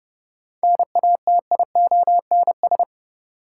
Morse code, heard as a keyed tone, about 30 wpm, 700 Hz. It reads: NATIONS
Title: Sets of 3 Words - New Word Drill - 30wpm